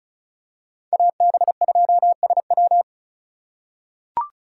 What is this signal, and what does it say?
Morse code, heard as a keyed tone, about 34 wpm, 690 Hz. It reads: AB2SW
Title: USA Call Signs - 5 Characters - 3x Character Spacing - 35wpm